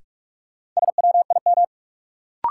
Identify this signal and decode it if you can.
Morse code, heard as a keyed tone, about 42 wpm, 700 Hz. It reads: SWIM
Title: Sentences from Top 1000 Words - Review Part 6 - New Word Drill - 45wpm